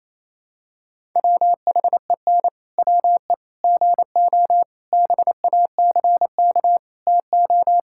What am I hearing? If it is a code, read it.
Morse code, heard as a keyed tone, about 28 wpm, 700 Hz. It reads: WHEN WE GO BACK TO
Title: Pangram Challenge v2 - 28wpm